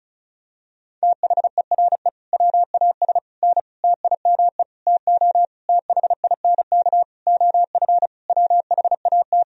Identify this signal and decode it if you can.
Morse code, heard as a keyed tone, about 34 wpm, 700 Hz. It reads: THERE WAS N TIME TO THINK OF WHAT
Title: Sentences from Top 400 Words and Abbreviations 35wpm